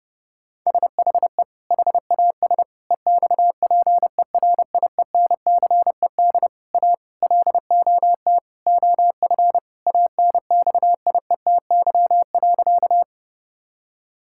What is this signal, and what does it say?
Morse code, heard as a keyed tone, about 30 wpm, 700 Hz. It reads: SHE HAS EXPERIENCED A LOT OF ANXIETY.
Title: Sentences from Top 1600 Words 30wpm